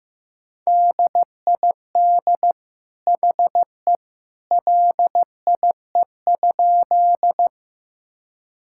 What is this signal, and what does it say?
Morse code, heard as a keyed tone, about 15 wpm, 700 Hz. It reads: DID HE LIE?
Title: Sentences from Top 1600 Words 15wpm